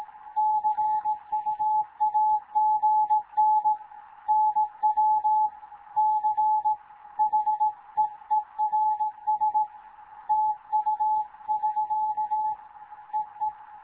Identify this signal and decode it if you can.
Morse code, heard as a keyed tone, about 18 wpm, 860 Hz. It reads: CUAGN NW CHEERS TU <SK> EE
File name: Morze-HA2MN / hangp / 2 11 GYAKORLAT